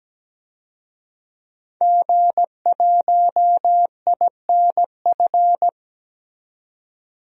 Morse code, heard as a keyed tone, about 17 wpm, 700 Hz. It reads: G1INF
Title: DXCC-100 Call Signs - 5 Characters - 17wpm